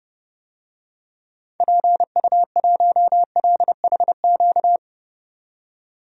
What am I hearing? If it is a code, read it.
Morse code, heard as a keyed tone, about 32 wpm, 700 Hz. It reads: PU1LHQ
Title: DXCC-100 Call Signs - 6 Characters - 30wpm